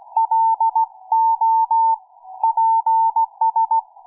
Morse code, heard as a keyed tone, about 16 wpm, spaced as for 14 wpm, 900 Hz. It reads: LOPS